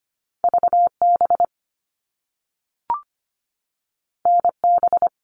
Morse code, heard as a keyed tone, about 25 wpm, 700 Hz. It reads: VB NB
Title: All Cut Numbers - 2-Digit 25wpm